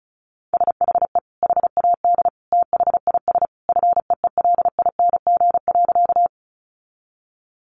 Morse code, heard as a keyed tone, about 34 wpm, 700 Hz. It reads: SHE HAD THIS FEELING.
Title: Sentences from Top 1200 Words 35wpm